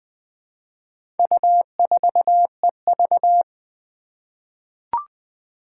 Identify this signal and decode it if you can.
Morse code, heard as a keyed tone, about 20 wpm, 700 Hz. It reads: U4EV
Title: Four Challenging Letter-Number Combinations 20wpm